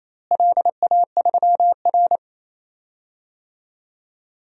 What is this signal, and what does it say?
Morse code, heard as a keyed tone, about 28 wpm, 700 Hz. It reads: LA3R
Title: DXCC-100 Call Signs - 4 Characters - Repeated 3x - 28wpm